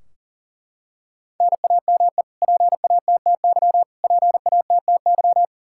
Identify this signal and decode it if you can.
Morse code, heard as a keyed tone, about 40 wpm, 700 Hz. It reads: NAME PATTY PATTY
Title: Practice QSO - Exchange 1 - Beginning - 40wpm